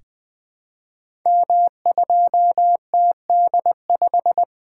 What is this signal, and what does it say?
Morse code, heard as a keyed tone, about 20 wpm, 700 Hz. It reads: M2TD5
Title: Code Groups - 5 Letter:Numbers - 20wpm